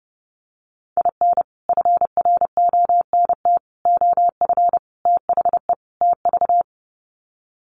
Morse code, heard as a keyed tone, about 30 wpm, 700 Hz. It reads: IN FRONT OF THE TV
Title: Sets of 5 Words Encore2 30wpm